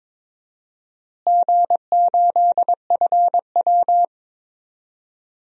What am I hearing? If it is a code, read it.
Morse code, heard as a keyed tone, about 22 wpm, 700 Hz. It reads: G8FW
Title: DXCC-100 Call Signs - 4 Characters - Repeated 3x - 22wpm